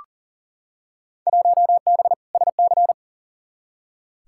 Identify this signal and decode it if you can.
Morse code, heard as a keyed tone, about 40 wpm, 700 Hz. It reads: 1B SC
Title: Field Day Exchanges - 40wpm